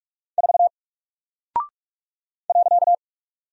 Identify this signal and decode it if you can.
Morse code, heard as a keyed tone, about 44 wpm, 700 Hz. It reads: 4 .
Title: Single Character (Letter:Number:Punctuation:Prosign) Alt 45wpm